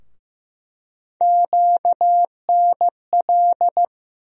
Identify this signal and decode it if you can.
Morse code, heard as a keyed tone, about 15 wpm, 700 Hz. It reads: QNL
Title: USA Call Sign Postfixes - 3 Letters 15wpm